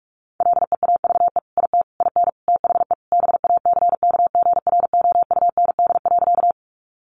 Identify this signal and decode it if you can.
Morse code, heard as a keyed tone, about 44 wpm, 700 Hz. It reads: LEAVE IT IN THE BACKGROUND.
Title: Sentences from Top 1300 Words 45wpm